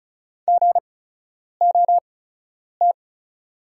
Morse code, heard as a keyed tone, about 36 wpm, spaced as for 10 wpm, 700 Hz. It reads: GOT
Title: 8x Character Spacing - Top 300 Words - 35wpm